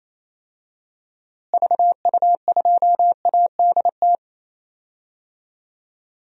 Morse code, heard as a keyed tone, about 28 wpm, 700 Hz. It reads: VU2ADT
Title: DXCC-100 Call Signs - 6 Characters - Repeated 2x - 28wpm